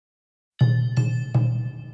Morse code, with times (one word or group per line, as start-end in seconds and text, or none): none